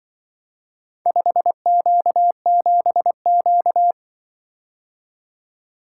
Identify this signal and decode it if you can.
Morse code, heard as a keyed tone, about 23 wpm, 700 Hz. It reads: 5Q7Q
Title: Speed-Racing - Four Letter-Number Combinations - 20wpm